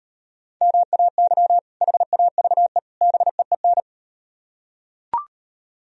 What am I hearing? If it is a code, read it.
Morse code, heard as a keyed tone, about 38 wpm, 700 Hz. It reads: MAY HAVE BEEN